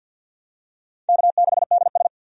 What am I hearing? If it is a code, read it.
Morse code, heard as a keyed tone, about 53 wpm, 700 Hz. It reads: K6DS
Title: USA Call Signs - 4 Characters - 50wpm